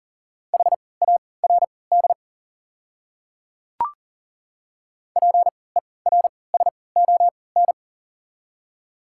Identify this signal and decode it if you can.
Morse code, heard as a keyed tone, about 40 wpm, spaced as for 23 wpm, 700 Hz. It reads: HARD PERSON
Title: 3x Character Spacing - Top 500 Words - 40wpm